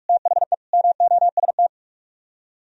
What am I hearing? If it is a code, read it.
Morse code, heard as a keyed tone, about 43 wpm, 700 Hz. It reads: THE MOST